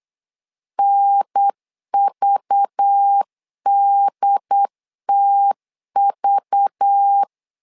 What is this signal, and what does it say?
Morse code, heard as a keyed tone, about 8 wpm, 800 Hz. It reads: NVDTV
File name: LEZIONI / 5 LBVC / 5 LBV 8wpm 2837